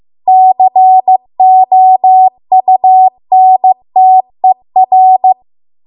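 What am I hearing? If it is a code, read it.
Morse code, heard as a keyed tone, about 15 wpm, 750 Hz. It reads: COUNTER
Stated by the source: Morse code practice at 15 wpm from the national amateur radio association's code archive.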